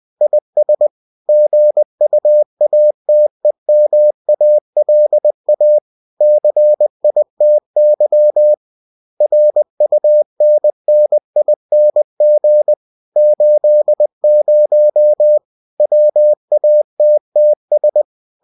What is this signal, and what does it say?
Morse code, heard as a keyed tone, about 20 wpm, 600 Hz. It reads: IS GUATEMALA CITY RUNNING 80 WATTS